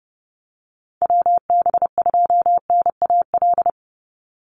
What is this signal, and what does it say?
Morse code, heard as a keyed tone, about 30 wpm, 700 Hz. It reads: WB2NAL
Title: USA Call Signs 30wpm